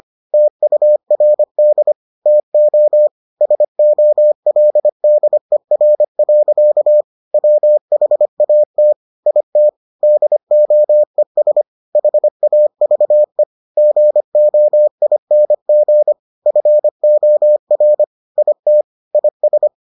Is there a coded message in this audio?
Yes: Morse code from 0.3 to 19.7 s, heard as a keyed tone, about 25 wpm, 600 Hz. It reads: TURD TO SOLDER. WHAT IT DOES HAVE GOING FOR IT IS